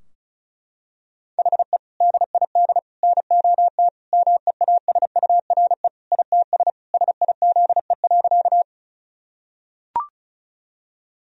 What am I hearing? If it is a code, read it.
Morse code, heard as a keyed tone, about 36 wpm, 700 Hz. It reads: HE DID NOT MEASURE ITS SIZE.
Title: Sentences from Top 1000 Words - Review Part 2 - 35wpm